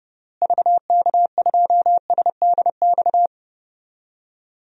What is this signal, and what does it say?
Morse code, heard as a keyed tone, about 31 wpm, 700 Hz. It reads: VK2SDX
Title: DXCC-100 Call Signs - 6 Characters - Repeated 3x - 30wpm